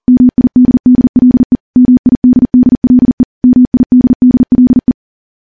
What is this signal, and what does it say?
Morse code, heard as a keyed tone, about 39 wpm, 250 Hz. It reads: MIDDLE MIDDLE MIDDLE
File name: morse / morse-training-40WPM